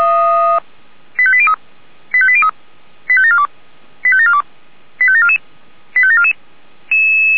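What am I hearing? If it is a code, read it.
Five-tone selective call (ZVEI), heard as quick runs of five tones, tones of about 70 ms. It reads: 85803 85803 85742 85742 85744 85744
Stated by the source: ZVEI five-tone selective call 85727 received on SDR.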